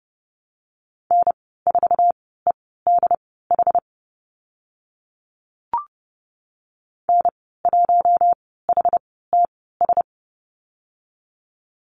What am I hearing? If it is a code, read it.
Morse code, heard as a keyed tone, about 30 wpm, spaced as for 17 wpm, 700 Hz. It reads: N4EDH N1HTS
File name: Lesson 15g - Number 5 - Call Signs 30wpm